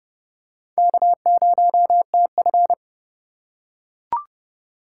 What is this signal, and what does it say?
Morse code, heard as a keyed tone, about 28 wpm, 700 Hz. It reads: K0TF